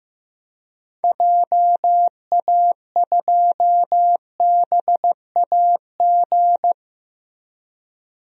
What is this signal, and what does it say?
Morse code, heard as a keyed tone, about 15 wpm, 700 Hz. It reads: JA2BAG
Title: DXCC-100 Call Signs - 6 Characters - Repeated 3x - 15wpm